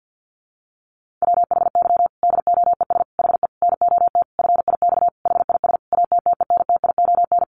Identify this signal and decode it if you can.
Morse code, heard as a keyed tone, about 50 wpm, 700 Hz. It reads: WHY DOES HE NOT FIX HIS ATTENTION
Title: Sentences from Top 1400 Words 50wpm